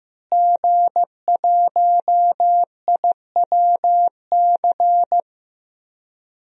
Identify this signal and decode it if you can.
Morse code, heard as a keyed tone, about 15 wpm, 700 Hz. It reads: G1IWC
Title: DXCC-100 Call Signs - 5 Characters - Repeated 3x - 15wpm